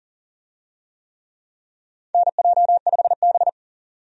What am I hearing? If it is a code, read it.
Morse code, heard as a keyed tone, about 37 wpm, 700 Hz. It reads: NJ5B
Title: USA Call Signs - 4 Characters - 40wpm